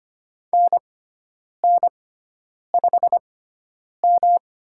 Morse code, heard as a keyed tone, about 25 wpm, spaced as for 9 wpm, 700 Hz. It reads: NN5M
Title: USA Call Signs - 6x Character Spacing - 25wpm